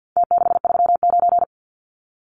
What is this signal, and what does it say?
Morse code, heard as a keyed tone, about 51 wpm, 700 Hz. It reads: T639
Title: Numbers - Contest Style - 4-Digit 50wpm